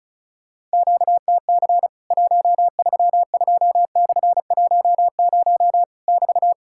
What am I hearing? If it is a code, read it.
Morse code, heard as a keyed tone, about 34 wpm, 700 Hz. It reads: QTC 132/10 <BT>
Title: Worked All Europe DX Contest - QTC Exchange - 35wpm